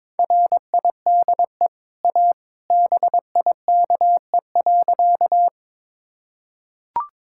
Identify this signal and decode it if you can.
Morse code, heard as a keyed tone, about 21 wpm, 700 Hz. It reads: RIDE A BIKE.
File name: practice / 22wpm / Easy Sentences with Hints - 22wpm